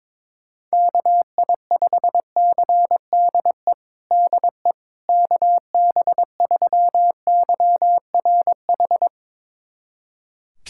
Morse code, heard as a keyed tone, about 22 wpm, 700 Hz. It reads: KI5CDE DE KB3YRH
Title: USA Callsign DE Callsign 22wpm